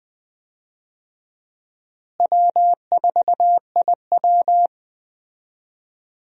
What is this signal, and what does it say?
Morse code, heard as a keyed tone, about 20 wpm, 700 Hz. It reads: W4IW
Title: USA Call Signs - 4 Characters - Repeated 2x - 20wpm